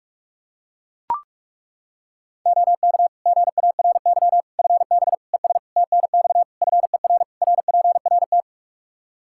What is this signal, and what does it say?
Morse code, heard as a keyed tone, about 41 wpm, 700 Hz. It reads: OK GARY FB ES TNX FER RPRT
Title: Practice QSO - Exchange 2 - Info - 45wpm